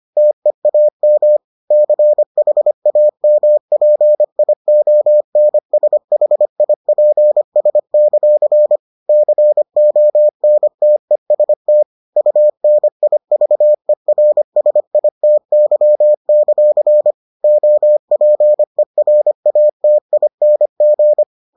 Morse code, heard as a keyed tone, about 25 wpm, 600 Hz. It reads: TEAM CHAMPIONSHIPS; CONTEST UNIVERSITY; OPERATING